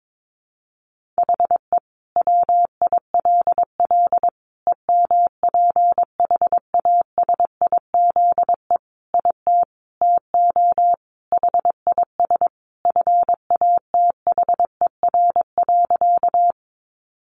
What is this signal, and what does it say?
Morse code, heard as a keyed tone, about 22 wpm, 700 Hz. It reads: HE WILL EMPHASIZE IT TO HIS FATHER.
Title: Sentences from Top 1500 Words 22wpm